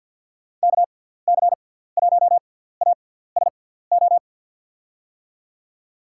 Morse code, heard as a keyed tone, about 47 wpm, spaced as for 17 wpm, 700 Hz. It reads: KC1ASO
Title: USA Call Signs - 6 Characters - 6x Character Spacing - 50wpm